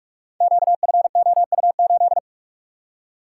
Morse code, heard as a keyed tone, about 44 wpm, 690 Hz. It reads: QFOU8
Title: Code Groups - 5 Letter:Numbers - 45wpm